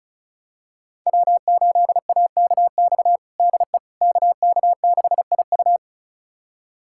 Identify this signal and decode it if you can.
Morse code, heard as a keyed tone, about 36 wpm, 690 Hz. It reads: W8AKX DE KK6IU
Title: USA Callsign DE Callsign - 5 Characters - 35wpm